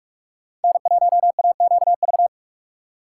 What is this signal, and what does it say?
Morse code, heard as a keyed tone, about 46 wpm, 690 Hz. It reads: N1AQV
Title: USA Call Signs - 4x Character Spacing - 45wpm